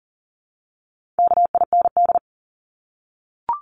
Morse code, heard as a keyed tone, about 40 wpm, 700 Hz. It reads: KIND
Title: Top 200 Words 40wpm